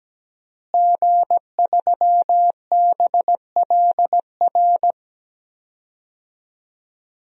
Morse code, heard as a keyed tone, about 17 wpm, 700 Hz. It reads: G3BLR
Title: DXCC-100 Call Signs - 5 Characters - Repeated 3x - 17wpm